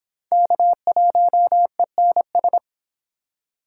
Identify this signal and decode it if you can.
Morse code, heard as a keyed tone, about 24 wpm, 700 Hz. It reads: K1ENS